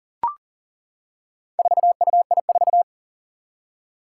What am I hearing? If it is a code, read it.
Morse code, heard as a keyed tone, about 40 wpm, 700 Hz. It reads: 4UI4
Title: Four Challenging Letter-Number Combinations 40wpm